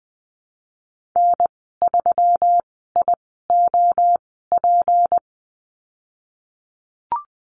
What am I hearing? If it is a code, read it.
Morse code, heard as a keyed tone, about 20 wpm, spaced as for 15 wpm, 700 Hz. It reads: N3IOP
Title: 193 Lesson 31g - Letter v - Call Signs 20wpm